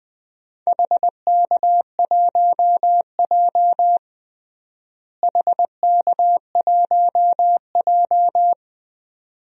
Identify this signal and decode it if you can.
Morse code, heard as a keyed tone, about 20 wpm, 700 Hz. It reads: HK1J HK1J
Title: DXCC-100 Call Signs - 4 Characters - Repeated 2x - 20wpm